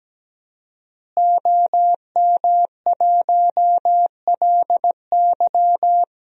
Morse code, heard as a keyed tone, about 17 wpm, 700 Hz. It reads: OM1LY
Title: DXCC-100 Call Signs - 5 Characters - Repeated 3x - 17wpm